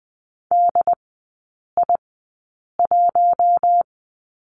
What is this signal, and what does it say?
Morse code, heard as a keyed tone, about 20 wpm, spaced as for 8 wpm, 700 Hz. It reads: DI1
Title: 067 Lesson 12e - Letter d - Three Characters 20wpm